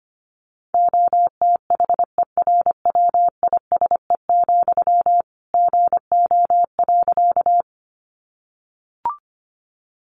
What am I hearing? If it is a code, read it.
Morse code, heard as a keyed tone, about 25 wpm, 700 Hz. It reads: OTHERWISE, GO.
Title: Sentences from Top 1200 Words 25wpm